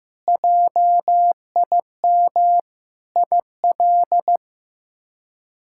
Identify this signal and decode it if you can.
Morse code, heard as a keyed tone, about 15 wpm, 700 Hz. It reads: JIM IL
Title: K1USN SST QSO Practice - 15wpm